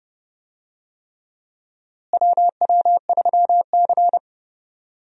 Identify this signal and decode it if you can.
Morse code, heard as a keyed tone, about 29 wpm, 700 Hz. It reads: WW3C
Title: USA Call Signs - 4 Characters - Repeated 3x - 30wpm